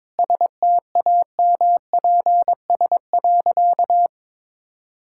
Morse code, heard as a keyed tone, about 22 wpm, 700 Hz. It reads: STAMPS.
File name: practice / 22wpm / Easy Sentences with Hints - 22wpm